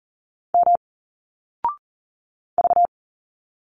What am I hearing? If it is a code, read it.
Morse code, heard as a keyed tone, about 40 wpm, 700 Hz. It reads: M V
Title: Single Letters 40wpm